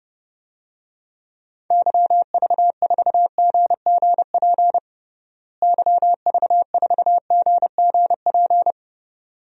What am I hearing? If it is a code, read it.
Morse code, heard as a keyed tone, about 30 wpm, 700 Hz. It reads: YV4GGP YV4GGP
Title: DXCC-100 Call Signs - 6 Characters - Repeated 2x - 30wpm